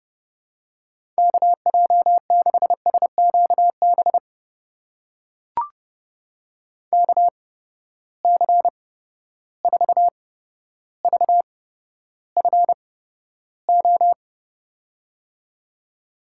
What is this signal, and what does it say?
Morse code, heard as a keyed tone, about 31 wpm, spaced as for 13 wpm, 700 Hz. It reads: KJ6SQB KC4VFO
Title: USA Call Signs - 6 Characters - 8x Character Spacing - 30wpm